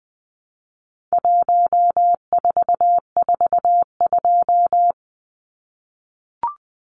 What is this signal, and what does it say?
Morse code, heard as a keyed tone, about 20 wpm, 700 Hz. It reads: 1442
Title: Numbers - 4-Digit 20wpm